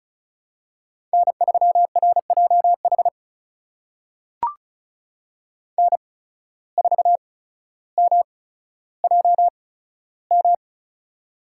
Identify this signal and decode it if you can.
Morse code, heard as a keyed tone, about 34 wpm, spaced as for 15 wpm, 700 Hz. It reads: N3RJH N4MJM
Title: USA Call Signs - 5 Characters - 8x Character Spacing - 35wpm